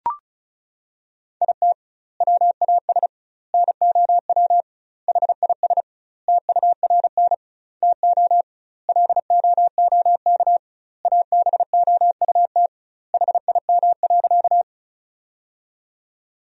Morse code, heard as a keyed tone, about 36 wpm, 700 Hz. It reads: IT WAS NOW HIS TURN TO LOOK ABOUT HIM.
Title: 2x Word-Spacing - Sentences from Top 200 Words - 35wpm